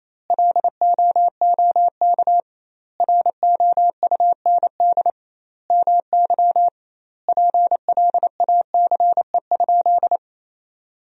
Morse code, heard as a keyed tone, about 28 wpm, 700 Hz. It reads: LOOK ROUND MY PLACE?
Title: 2x Word-Spacing - Sentences from Top 200 Words - 28wpm